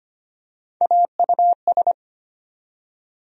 Morse code, heard as a keyed tone, about 25 wpm, 700 Hz. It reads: AUS
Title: USA Call Sign Postfixes - 3 Letters 25wpm